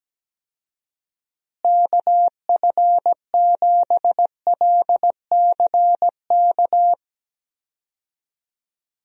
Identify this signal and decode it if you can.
Morse code, heard as a keyed tone, about 17 wpm, 700 Hz. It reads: KF7LCK
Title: USA Call Signs - 6 Characters - Repeated 3x - 17wpm